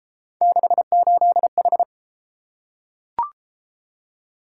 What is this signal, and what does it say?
Morse code, heard as a keyed tone, about 30 wpm, 700 Hz. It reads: 68H